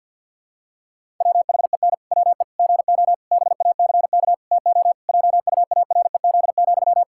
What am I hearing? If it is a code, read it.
Morse code, heard as a keyed tone, about 50 wpm, 700 Hz. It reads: WHEN WE GO BACK TO JUAREZ,
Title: Pangram Challenge v2 - 50wpm